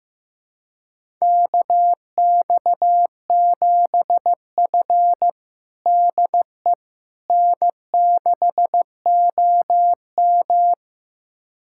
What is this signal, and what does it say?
Morse code, heard as a keyed tone, about 16 wpm, 700 Hz. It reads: KX7F DE N6OM
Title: USA Callsign DE Callsign - 4 Characters - 15wpm